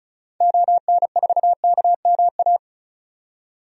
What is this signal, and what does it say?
Morse code, heard as a keyed tone, about 36 wpm, 700 Hz. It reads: ON4KMA
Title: DXCC-100 Call Signs - 6 Characters - 35wpm